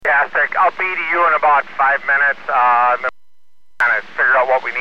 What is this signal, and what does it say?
Emergency Alert System (SAME) tones (header bursts), heard as harsh buzzing bursts, with speech. It is heard from 0.1 to 4.4 s.